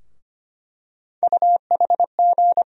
Morse code, heard as a keyed tone, about 25 wpm, 700 Hz. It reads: UHG
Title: USA Call Sign Postfixes - 3 Letters 25wpm